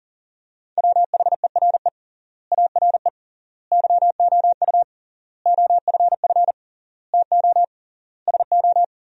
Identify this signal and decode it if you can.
Morse code, heard as a keyed tone, about 40 wpm, 700 Hz. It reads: WHERE ARE YOU OFF TO SO
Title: 3x Word-Spacing - Sentences from Top 400 Words - 40wpm